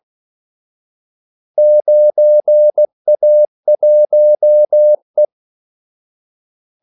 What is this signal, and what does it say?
Morse code, heard as a keyed tone, about 16 wpm, 600 Hz. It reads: 9A1E
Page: Kaldesignaler 16 wpm